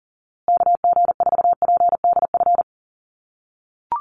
Morse code, heard as a keyed tone, about 40 wpm, 700 Hz. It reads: KG4PDF
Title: USA Call Signs Repeated 3x 40wpm